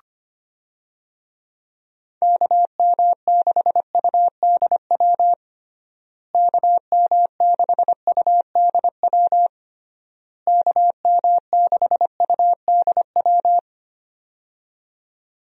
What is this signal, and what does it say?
Morse code, heard as a keyed tone, about 25 wpm, 700 Hz. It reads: KM6UDW KM6UDW KM6UDW
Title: USA Call Signs - 6 Characters - Repeated 3x - 25wpm